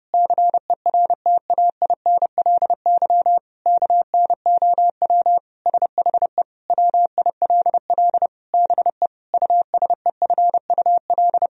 Morse code, heard as a keyed tone, about 30 wpm, 700 Hz. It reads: CERTAINLY KNOW SHE WILL BE USEFUL
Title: Sentences from Top 1000 Words - Review Part 4 - 30wpm